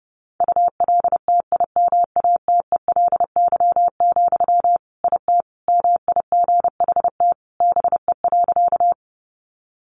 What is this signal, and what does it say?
Morse code, heard as a keyed tone, about 30 wpm, 700 Hz. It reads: ULTIMATELY, IT MIGHT BE.
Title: Sentences from Top 1200 Words 30wpm